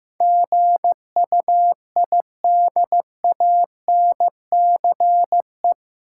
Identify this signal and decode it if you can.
Morse code, heard as a keyed tone, about 15 wpm, 700 Hz. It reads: GUIDANCE
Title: Sentences from Top 1600 Words - New Word Drill - 15wpm